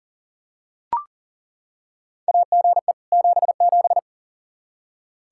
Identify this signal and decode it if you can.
Morse code, heard as a keyed tone, about 40 wpm, 700 Hz. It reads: AGE 77
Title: Practice QSO - Exchange 3 - More Info - 40wpm